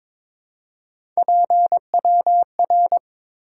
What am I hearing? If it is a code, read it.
Morse code, heard as a keyed tone, about 22 wpm, 690 Hz. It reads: PWR
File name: Top 90 QSO Elements - 4x Character Spacing - 22wpm